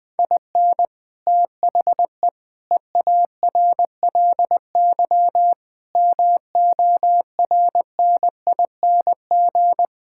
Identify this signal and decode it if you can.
Morse code, heard as a keyed tone, about 21 wpm, 700 Hz. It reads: IN THE EARLY MORNING
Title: Sentences with 6-Word Phrases - 20wpm